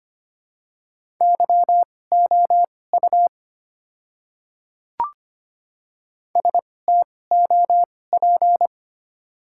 Morse code, heard as a keyed tone, about 25 wpm, spaced as for 18 wpm, 690 Hz. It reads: YOU STOP
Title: 2x Character Spacing - Top 300 Words - 25wpm